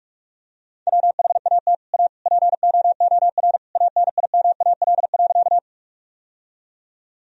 Morse code, heard as a keyed tone, about 41 wpm, 700 Hz. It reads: WHAT A POOR ANIMAL.
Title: Sentences from Top 1000 Words - Review Part 1 - 45wpm